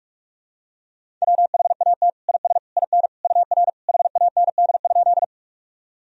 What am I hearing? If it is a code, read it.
Morse code, heard as a keyed tone, about 41 wpm, 700 Hz. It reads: WHAT IS IN UR HAND?
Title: Sentences from Top 200 Words and Abbreviations 45wpm